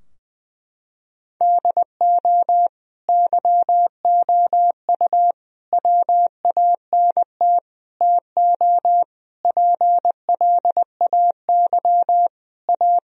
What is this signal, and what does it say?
Morse code, heard as a keyed tone, about 20 wpm, 700 Hz. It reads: DO YOU WANT TO PLAY A
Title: Sentences from Top 1000 Words - Review Part 3 - 20wpm